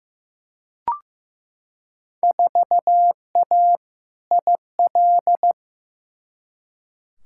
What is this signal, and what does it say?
Morse code, heard as a keyed tone, about 15 wpm, 700 Hz. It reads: 4A IL
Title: Field Day Exchanges - 15wpm